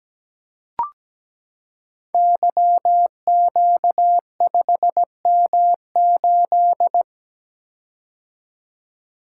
Morse code, heard as a keyed tone, about 17 wpm, 700 Hz. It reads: YQ5M8
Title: Code Groups - 5 Letter:Numbers - 17wpm